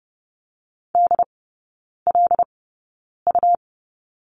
Morse code, heard as a keyed tone, about 30 wpm, spaced as for 9 wpm, 700 Hz. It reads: DLU
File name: Lesson 18e - Letter m - Three Characters 30wpm